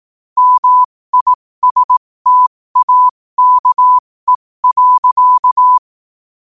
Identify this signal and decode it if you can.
Morse code, heard as a keyed tone, about 18 wpm, spaced as for 15 wpm, 1000 Hz. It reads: MISTAKE.
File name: qotdfiles / 15wpm